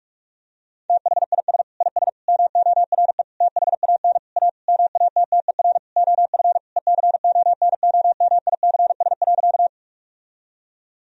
Morse code, heard as a keyed tone, about 42 wpm, 700 Hz. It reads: THIS IS MORE THAN A MATTER OF ECONOMICS.
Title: Sentences from Top 1000 Words - Review Part 3 - 45wpm